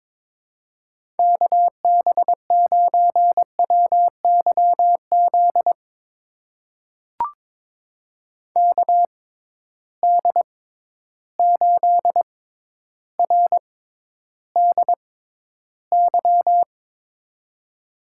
Morse code, heard as a keyed tone, about 22 wpm, spaced as for 11 wpm, 700 Hz. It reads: KB9WYZ KD8RDY
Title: USA Call Signs - 6 Characters - 6x Character Spacing - 22wpm